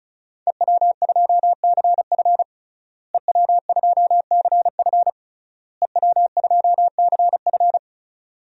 Morse code, heard as a keyed tone, about 34 wpm, 700 Hz. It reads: EW2CF EW2CF EW2CF
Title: DXCC-100 Call Signs - 5 Characters - Repeated 3x - 35wpm